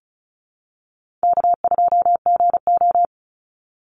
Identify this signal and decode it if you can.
Morse code, heard as a keyed tone, about 34 wpm, 700 Hz. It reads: K2GO
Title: Call Signs Repeated 2x - 35wpm